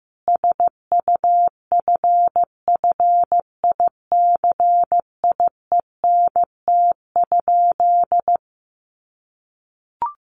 Morse code, heard as a keyed tone, about 15 wpm, 700 Hz. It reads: SUFFICIENT?
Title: Sentences from Top 1400 Words 15wpm